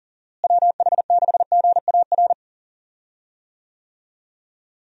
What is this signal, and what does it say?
Morse code, heard as a keyed tone, about 40 wpm, 700 Hz. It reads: WH6GAR
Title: USA Call Signs - 6 Characters - Repeated 3x - 40wpm